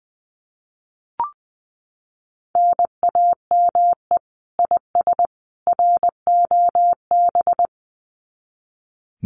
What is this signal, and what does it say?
Morse code, heard as a keyed tone, about 20 wpm, 700 Hz. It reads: NAME IS ROB
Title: Common QSO Phrases 20wpm